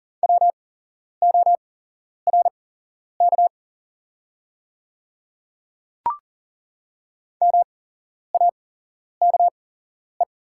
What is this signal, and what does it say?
Morse code, heard as a keyed tone, about 39 wpm, spaced as for 11 wpm, 700 Hz. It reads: WORK MAKE
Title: 8x Character Spacing - Top 300 Words - 40wpm